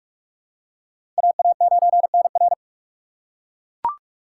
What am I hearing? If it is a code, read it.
Morse code, heard as a keyed tone, about 42 wpm, 700 Hz. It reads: AA9NR
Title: USA Call Signs - 5 Characters - 6x Character Spacing - 45wpm